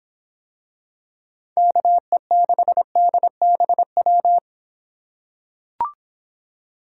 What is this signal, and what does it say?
Morse code, heard as a keyed tone, about 25 wpm, 700 Hz. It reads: KE6DBW